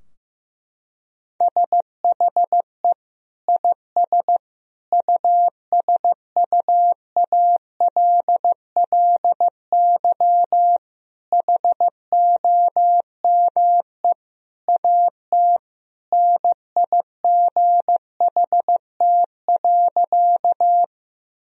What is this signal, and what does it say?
Morse code, heard as a keyed tone, about 15 wpm, 700 Hz. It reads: SHE IS USUALLY HOME AT NIGHT.
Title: Sentences from Top 1000 Words - Review Part 3 - 15wpm